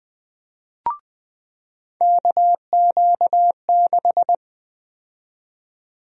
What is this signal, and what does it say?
Morse code, heard as a keyed tone, about 20 wpm, 700 Hz. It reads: KQ6
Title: USA Call Sign Prefixes - 20wpm